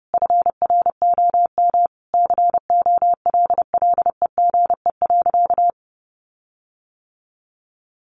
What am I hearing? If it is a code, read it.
Morse code, heard as a keyed tone, about 30 wpm, 700 Hz. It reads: FROM COLLEGE.
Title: Sentences from Top 1300 Words 30wpm